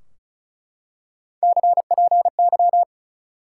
Morse code, heard as a keyed tone, about 36 wpm, 700 Hz. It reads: CPY
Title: Top 180 QSO Elements 35wpm